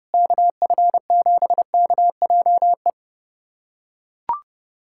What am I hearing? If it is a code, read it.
Morse code, heard as a keyed tone, about 32 wpm, 700 Hz. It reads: KF7KJE
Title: USA Callsign DE Callsign - 6 Characters - 30wpm